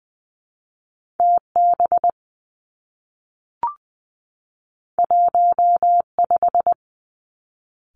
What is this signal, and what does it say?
Morse code, heard as a keyed tone, about 20 wpm, 700 Hz. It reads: T B 1 5
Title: Two Challenging Letter-Number Combinations 20wpm